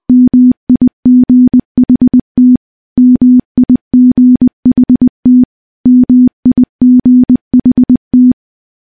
Morse code, heard as a keyed tone, about 20 wpm, 250 Hz. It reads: MIGHT MIGHT MIGHT
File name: morse-training-20WPM